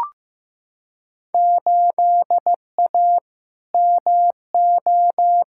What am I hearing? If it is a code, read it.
Morse code, heard as a keyed tone, about 15 wpm, 700 Hz. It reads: 8A MO
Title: Field Day Exchanges Encore - 15wpm